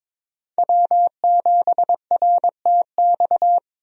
Morse code, heard as a keyed tone, about 22 wpm, 700 Hz. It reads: W7RTX
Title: USA Call Signs - 5 Characters - 22wpm